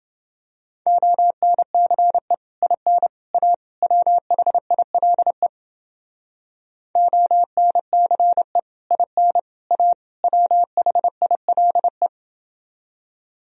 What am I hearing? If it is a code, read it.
Morse code, heard as a keyed tone, about 29 wpm, 700 Hz. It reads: ONCE IN A WHILE ONCE IN A WHILE